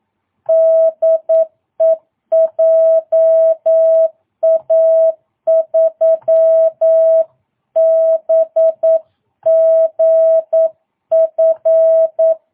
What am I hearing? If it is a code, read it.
Morse code, heard as a keyed tone, about 9 wpm, 650 Hz. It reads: DE JA3BGF
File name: jh8cbh / 2022cw